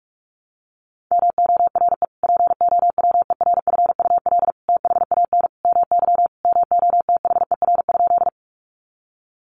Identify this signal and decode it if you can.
Morse code, heard as a keyed tone, about 46 wpm, 700 Hz. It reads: MORE POWERFUL THAN MY MOTHER?
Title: Sentences from Top 1300 Words 45wpm